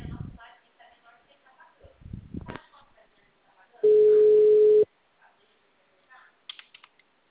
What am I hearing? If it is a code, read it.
Ringback tone (Europe, 425 Hz), with speech.
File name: out- -2006- - - .0